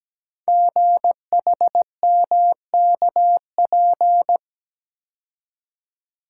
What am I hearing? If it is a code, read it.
Morse code, heard as a keyed tone, about 17 wpm, 700 Hz. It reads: GHMKP
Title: Code Groups - 5 Letters - 17wpm